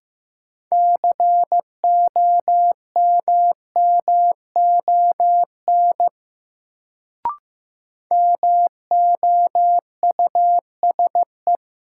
Morse code, heard as a keyed tone, about 15 wpm, 700 Hz. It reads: COMMON MOUSE
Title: Sentences from Top 700 Words - New Word Drill - 15wpm